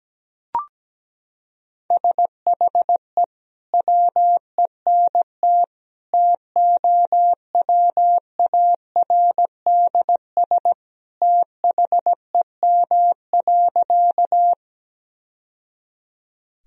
Morse code, heard as a keyed tone, about 17 wpm, 700 Hz. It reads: SHE WENT TOWARDS THEM.
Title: Sentences from Top 1000 Words - Review Part 2 - 17wpm